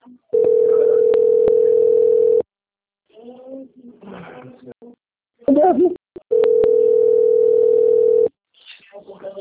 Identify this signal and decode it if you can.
Ringback tone (North America), 440 + 480 Hz, with speech.
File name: out- - - - - .24784